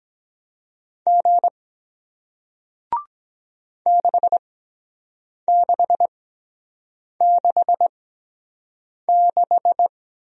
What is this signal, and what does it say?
Morse code, heard as a keyed tone, about 23 wpm, spaced as for 7 wpm, 700 Hz. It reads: G 6666